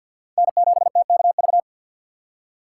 Morse code, heard as a keyed tone, about 49 wpm, 700 Hz. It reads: N7TKV
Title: USA Call Signs - 5 Characters - 6x Character Spacing - 50wpm